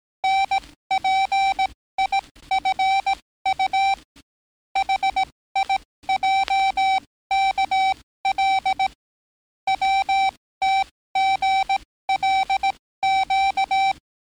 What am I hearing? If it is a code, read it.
Morse code, heard as a keyed tone, about 18 wpm, spaced as for 14 wpm, 770 Hz. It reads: NPIFU HIJKL WTGLQ
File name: training / NAVPERS 11049to11060 / NAVPERS 11056 RA RB